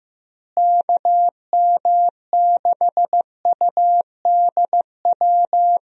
Morse code, heard as a keyed tone, about 15 wpm, 700 Hz. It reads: KM6UDW
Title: USA Call Signs - 6 Characters - Repeated 3x - 15wpm